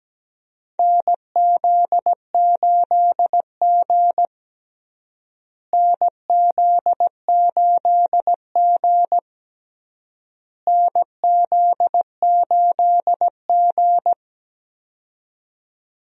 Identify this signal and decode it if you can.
Morse code, heard as a keyed tone, about 17 wpm, 700 Hz. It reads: NZ8G NZ8G NZ8G
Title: USA Call Signs - 4 Characters - Repeated 3x - 17wpm